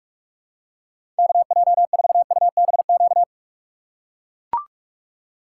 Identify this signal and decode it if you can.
Morse code, heard as a keyed tone, about 43 wpm, 700 Hz. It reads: KJ4UBQ
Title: USA Call Signs - 6 Characters - 2x Character Spacing - 45wpm